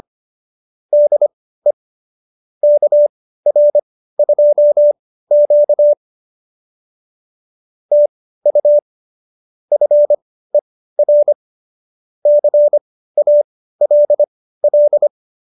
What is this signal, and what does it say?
Morse code, heard as a keyed tone, about 25 wpm, spaced as for 15 wpm, 590 Hz. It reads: DE KR2Q TU FER CALL
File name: qso220b 15